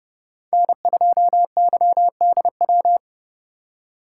Morse code, heard as a keyed tone, about 38 wpm, 700 Hz. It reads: N2YDW